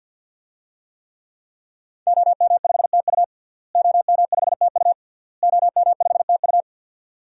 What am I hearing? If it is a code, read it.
Morse code, heard as a keyed tone, about 51 wpm, 700 Hz. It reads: OM5TU OM5TU OM5TU
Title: DXCC-100 Call Signs - 5 Characters - Repeated 3x - 50wpm